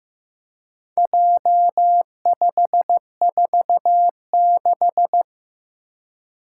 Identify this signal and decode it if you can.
Morse code, heard as a keyed tone, about 15 wpm, 700 Hz. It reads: J546
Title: Four Challenging Letter-Number Combinations 15wpm